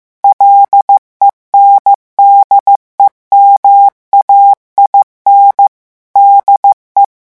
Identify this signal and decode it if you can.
Morse code, heard as a keyed tone, about 15 wpm, 800 Hz. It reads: LENDEMAIN DE
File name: morse mp3 / revision9 citations humoristiques vitesse 14 mots minute